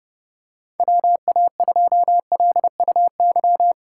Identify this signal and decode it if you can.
Morse code, heard as a keyed tone, about 32 wpm, 700 Hz. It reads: WA2LUY
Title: USA Call Signs - 6 Characters - 6x Character Spacing - 30wpm